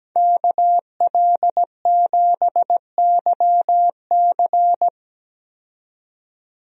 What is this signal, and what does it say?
Morse code, heard as a keyed tone, about 17 wpm, 700 Hz. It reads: KL7YC
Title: DXCC-100 Call Signs - 5 Characters - 17wpm